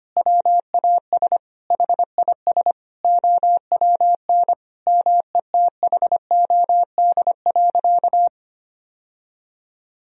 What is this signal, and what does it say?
Morse code, heard as a keyed tone, about 25 wpm, 700 Hz. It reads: WAS HIS OWN METHOD.
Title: Sentences from Top 1000 Words - Review Part 3 - 25wpm